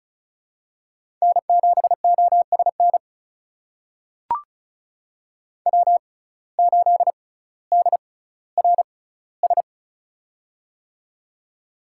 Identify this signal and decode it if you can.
Morse code, heard as a keyed tone, about 36 wpm, spaced as for 18 wpm, 700 Hz. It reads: N7OSN W8DRS
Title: USA Call Signs - 5 Characters - 6x Character Spacing - 35wpm